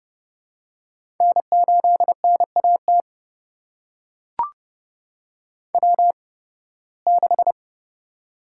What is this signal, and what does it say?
Morse code, heard as a keyed tone, about 30 wpm, spaced as for 20 wpm, 700 Hz. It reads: N8NAT W6
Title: USA Call Signs - 5 Characters - 8x Character Spacing - 30wpm